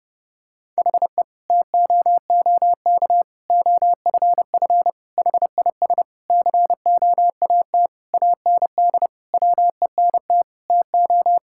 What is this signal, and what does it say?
Morse code, heard as a keyed tone, about 30 wpm, 700 Hz. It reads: HE TOOK OFF HIS COAT AND WENT TO
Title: Sentences from Top 1600 Words 30wpm